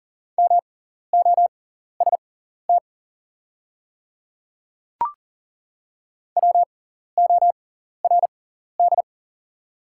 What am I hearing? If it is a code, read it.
Morse code, heard as a keyed tone, about 40 wpm, spaced as for 14 wpm, 700 Hz. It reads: MOST WORD
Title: 6x Character Spacing - Top 100 Words - 40wpm